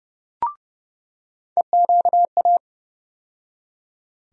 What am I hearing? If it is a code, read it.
Morse code, heard as a keyed tone, about 31 wpm, 700 Hz. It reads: EQA
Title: USA Call Sign Postfixes - 3 Letters 30wpm